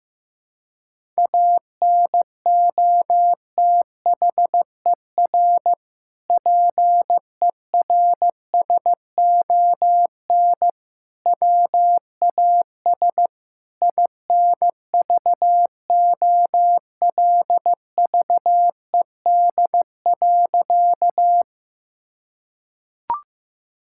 Morse code, heard as a keyed tone, about 15 wpm, 700 Hz. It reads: ANOTHER PERSON WAS INVOLVED.
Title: Sentences from Top 1000 Words - Review Part 5 - 15wpm